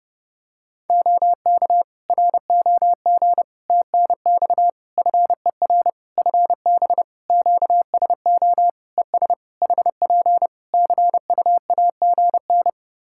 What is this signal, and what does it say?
Morse code, heard as a keyed tone, about 32 wpm, 700 Hz. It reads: OK ROG TNX FER FB QSO ES HP CUAGN
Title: Practice QSO - Exchange 4 - Ending - 30wpm